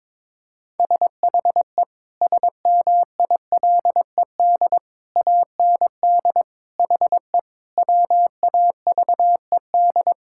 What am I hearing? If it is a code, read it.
Morse code, heard as a keyed tone, about 22 wpm, 700 Hz. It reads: SHE SMILED AND HE WAVED
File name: practice / 22wpm / Sentences with 2-Word Phrases - 22wpm